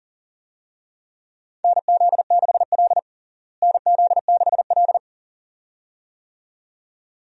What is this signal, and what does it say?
Morse code, heard as a keyed tone, about 40 wpm, 700 Hz. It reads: NZ6L NZ6L
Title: USA Call Signs - 4 Characters - Repeated 2x - 40wpm